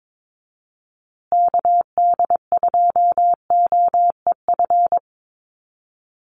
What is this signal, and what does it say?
Morse code, heard as a keyed tone, about 22 wpm, 700 Hz. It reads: KD2OEF
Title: USA Call Signs Repeated 3x 22wpm